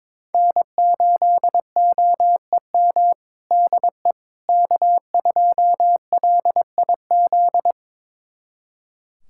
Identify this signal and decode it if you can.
Morse code, heard as a keyed tone, about 22 wpm, 700 Hz. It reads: N8OEM DE K2LIZ
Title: USA Callsign DE Callsign 22wpm